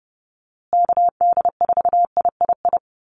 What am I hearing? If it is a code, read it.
Morse code, heard as a keyed tone, about 30 wpm, 700 Hz. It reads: KD4III
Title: USA Call Signs Repeated 2x 30wpm